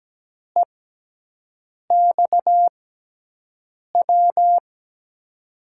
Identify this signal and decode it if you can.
Morse code, heard as a keyed tone, about 17 wpm, spaced as for 6 wpm, 700 Hz. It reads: EXW
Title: USA Call Signs - 6x Character Spacing - 17wpm